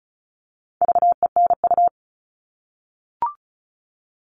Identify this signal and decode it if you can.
Morse code, heard as a keyed tone, about 34 wpm, 700 Hz. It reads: VENU
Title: All Cut Numbers - 4-Digit 35wpm